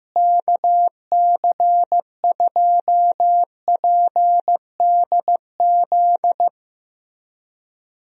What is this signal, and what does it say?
Morse code, heard as a keyed tone, about 14 wpm, 700 Hz. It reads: KC2PDZ
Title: USA Call Signs - 6 Characters - Repeated 3x - 15wpm